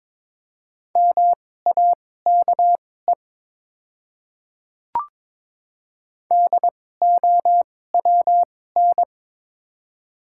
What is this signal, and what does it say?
Morse code, heard as a keyed tone, about 22 wpm, spaced as for 16 wpm, 700 Hz. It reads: MAKE DOWN
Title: 2x Character Spacing - Top 100 Words - 22wpm